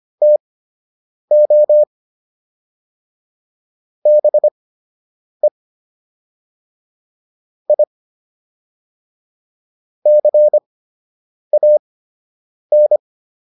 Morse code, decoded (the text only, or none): TO BE I CAN